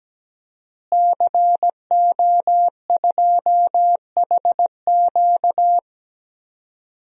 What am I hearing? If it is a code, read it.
Morse code, heard as a keyed tone, about 17 wpm, 700 Hz. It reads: CO2HQ
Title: DXCC-100 Call Signs - 5 Characters - Repeated 2x - 17wpm